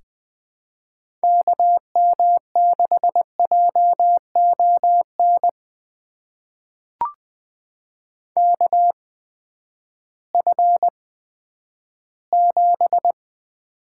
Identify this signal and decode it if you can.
Morse code, heard as a keyed tone, about 20 wpm, spaced as for 11 wpm, 700 Hz. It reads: KM6JON KF7
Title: USA Call Signs - 6 Characters - 8x Character Spacing - 20wpm